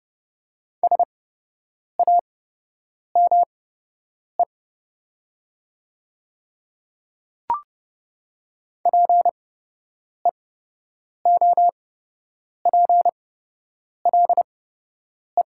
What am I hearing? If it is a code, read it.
Morse code, heard as a keyed tone, about 30 wpm, spaced as for 8 wpm, 700 Hz. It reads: SAME PEOPLE
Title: 8x Character Spacing - Top 200 Words - 30wpm